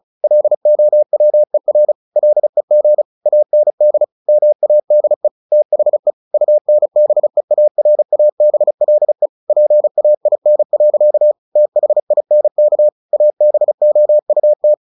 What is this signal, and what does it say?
Morse code, heard as a keyed tone, about 34 wpm, 600 Hz. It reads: LOWER LEG AND MADE THE UNBEARABLE PAIN. THINK ABOUT
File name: the old man and the sea 2 35wpm